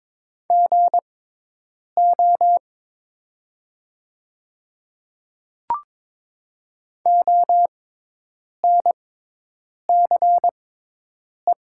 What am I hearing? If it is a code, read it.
Morse code, heard as a keyed tone, about 22 wpm, spaced as for 8 wpm, 700 Hz. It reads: GO ONCE
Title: 6x Character Spacing - Top 300 Words - 22wpm